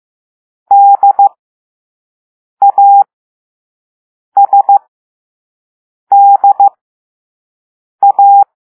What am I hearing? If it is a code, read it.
Morse code, heard as a keyed tone, about 15 wpm, spaced as for 5 wpm, 800 Hz. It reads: DASDA